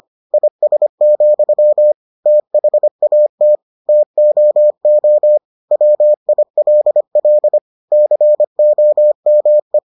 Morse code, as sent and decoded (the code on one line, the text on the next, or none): .. ... --..-- / - .... .- - / - --- --- / .-- .. .-.. .-.. / -.-. --- -- .
IS, THAT TOO WILL COME